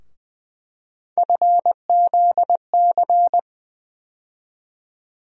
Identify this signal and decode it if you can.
Morse code, heard as a keyed tone, about 20 wpm, 700 Hz. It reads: FZC
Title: USA Call Sign Postfixes - 3 Letters 20wpm